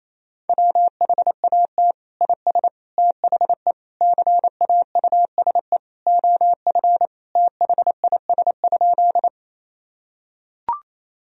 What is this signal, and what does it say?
Morse code, heard as a keyed tone, about 28 wpm, 700 Hz. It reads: WHAT IS THE CAUSE OF THIS?
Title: Sentences from Top 1000 Words - Review Part 1 - 28wpm